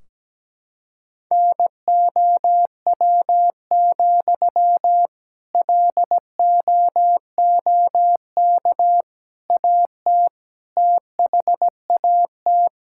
Morse code, heard as a keyed tone, about 17 wpm, 700 Hz. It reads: NOW, LOOK AT THAT
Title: Sentences from Top 1600 Words 17wpm